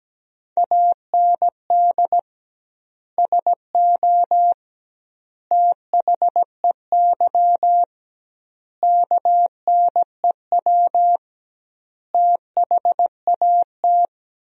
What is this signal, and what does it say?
Morse code, heard as a keyed tone, about 17 wpm, 700 Hz. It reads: AND SO THEY KNEW THAT
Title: 2x Word-Spacing - Sentences from Top 400 Words - 17wpm